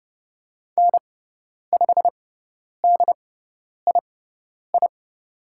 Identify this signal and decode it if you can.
Morse code, heard as a keyed tone, about 29 wpm, spaced as for 10 wpm, 690 Hz. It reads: N5DII
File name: Lesson 17g - Letter u - Call Signs 30wpm